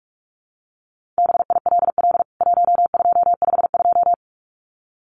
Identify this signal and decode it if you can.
Morse code, heard as a keyed tone, about 44 wpm, 700 Hz. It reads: BILL 1252
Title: CWT Exchange 45wpm